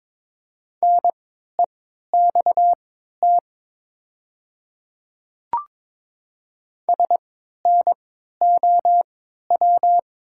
Morse code, heard as a keyed tone, about 22 wpm, spaced as for 12 wpm, 690 Hz. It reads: NEXT SNOW
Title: 3x Character Spacing - Top 500 Words - 22wpm